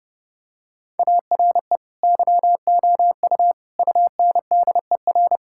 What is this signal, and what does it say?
Morse code, heard as a keyed tone, about 30 wpm, 700 Hz. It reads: ARE YOU UNDER
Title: Sentences from Top 1600 Words 30wpm